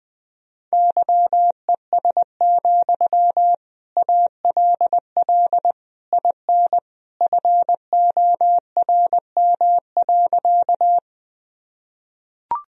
Morse code, heard as a keyed tone, about 20 wpm, 700 Hz. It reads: YES, ALL IN FORM.
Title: Sentences from Top 1000 Words - Review Part 1 - 20wpm